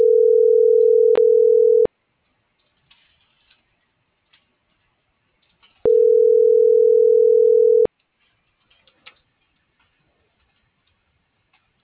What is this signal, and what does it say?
Ringback tone (North America), 440 + 480 Hz.